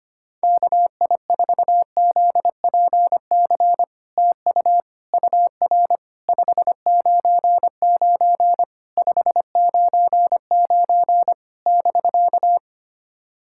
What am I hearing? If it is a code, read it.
Morse code, heard as a keyed tone, about 25 wpm, 700 Hz. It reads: KI4ZPC TU UR 599 599 <BK>
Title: SOTA QSO Practice 25wpm